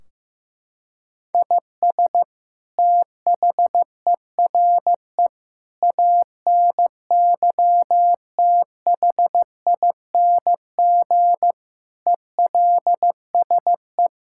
Morse code, heard as a keyed tone, about 15 wpm, 700 Hz. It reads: IS THERE ANYTHING ELSE
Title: Sentences from Top 1600 Words 15wpm